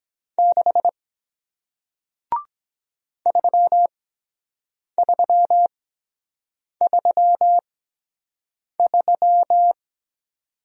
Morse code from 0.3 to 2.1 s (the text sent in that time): 6